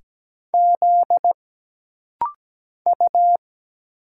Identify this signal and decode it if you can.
Morse code, heard as a keyed tone, about 17 wpm, 700 Hz. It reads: Z U
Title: Single Letters - Rapid-Fire 17wpm